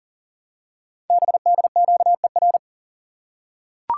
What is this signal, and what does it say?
Morse code, heard as a keyed tone, about 40 wpm, 700 Hz. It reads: BDQER
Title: Code Groups - 5 Letters - 40wpm